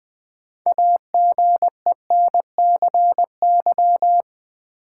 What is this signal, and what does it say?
Morse code, heard as a keyed tone, about 20 wpm, 700 Hz. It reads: AGENCY
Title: Sentences from Top 1300 Words - New Word Drill Alt - 20wpm